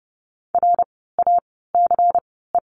Morse code, heard as a keyed tone, about 30 wpm, spaced as for 17 wpm, 700 Hz. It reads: RACE
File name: Lesson 16f - Letter c - Words 30wpm